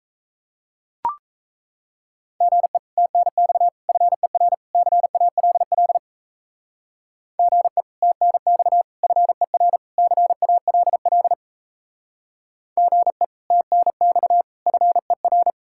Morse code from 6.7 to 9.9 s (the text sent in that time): GE TNX FER